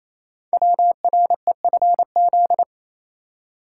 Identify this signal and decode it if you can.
Morse code, heard as a keyed tone, about 27 wpm, 700 Hz. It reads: WREFZ
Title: Code Groups - 5 Letter:Numbers - 28wpm